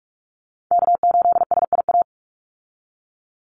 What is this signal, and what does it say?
Morse code, heard as a keyed tone, about 44 wpm, 700 Hz. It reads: K8SIA
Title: Call Signs Repeated 3x - 45wpm